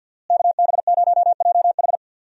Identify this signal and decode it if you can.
Morse code, heard as a keyed tone, about 52 wpm, 700 Hz. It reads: KB0JH
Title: USA Call Signs - 5 Characters - Repeated 3x - 50wpm